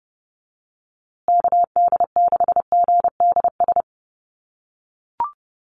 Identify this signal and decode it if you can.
Morse code, heard as a keyed tone, about 29 wpm, 700 Hz. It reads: KD6GDS
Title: USA Call Signs Repeated 3x 30wpm